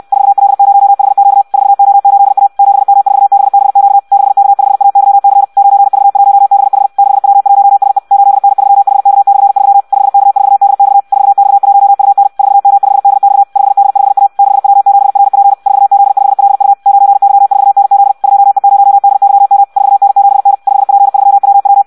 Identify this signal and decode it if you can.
Morse code, heard as a keyed tone, about 64 wpm, 800 Hz. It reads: 4L0HQ 4O8A 7M4BBQ 6K5AYC 9V1BH 6W1SE 8N3HK/3 5K4DX 4X1UN 4M5DX 5D5A 6W7RP 4Z5LU 9Y4NZ 2E0DYM 4N7N 5P3WW
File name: Audio practica CW / qrq-tx-66 2